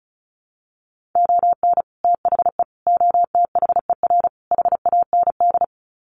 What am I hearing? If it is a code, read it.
Morse code, heard as a keyed tone, about 34 wpm, 700 Hz. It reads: ON THE OTHER HAND
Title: Sets of 4 Words 35wpm